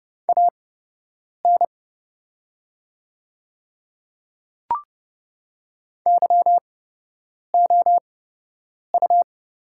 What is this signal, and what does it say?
Morse code, heard as a keyed tone, about 30 wpm, spaced as for 8 wpm, 700 Hz. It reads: AN YOU
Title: 8x Character Spacing - Top 100 Words - 30wpm